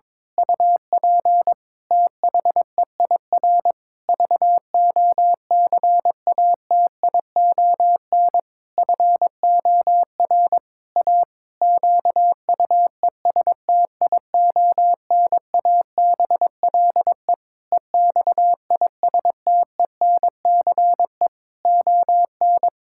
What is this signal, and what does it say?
Morse code, heard as a keyed tone, about 22 wpm, 700 Hz. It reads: UP THEIR VOCATION FOR A QUESTIONABLE EXISTENCE ON
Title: Pangram Challenge v2 - 22wpm